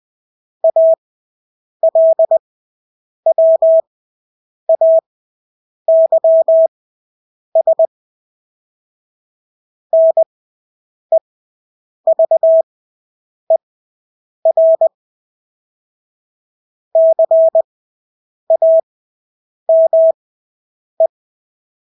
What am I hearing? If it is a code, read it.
Morse code, heard as a keyed tone, about 20 wpm, spaced as for 8 wpm, 650 Hz. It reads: ALWAYS NEVER CAME
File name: Engelske ord 20/8 wpm